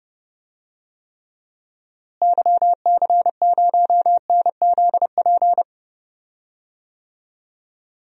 Morse code, heard as a keyed tone, about 31 wpm, 700 Hz. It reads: YC0NZP
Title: DXCC-100 Call Signs - 6 Characters - Repeated 2x - 30wpm